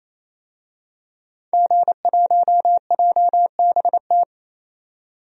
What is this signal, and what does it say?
Morse code, heard as a keyed tone, about 28 wpm, 700 Hz. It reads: G1JBT
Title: DXCC-100 Call Signs - 5 Characters - 28wpm